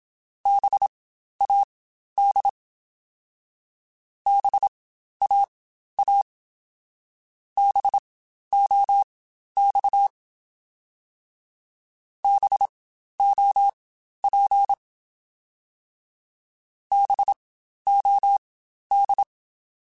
Morse code, heard as a keyed tone, about 27 wpm, spaced as for 12 wpm, 800 Hz. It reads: BAD BAA BOX BOP BOD
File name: Spaced Alphas / Spaced Alphas mp3s / Bs sp @25